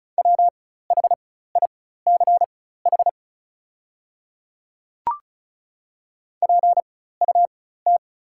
Morse code, heard as a keyed tone, about 36 wpm, spaced as for 16 wpm, 690 Hz. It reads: WHICH PUT
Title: 4x Character Spacing - Top 300 Words - 35wpm